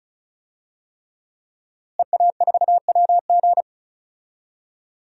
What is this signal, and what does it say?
Morse code, heard as a keyed tone, about 34 wpm, 700 Hz. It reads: EA4WG
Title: DXCC-100 Call Signs - 5 Characters - Repeated 3x - 35wpm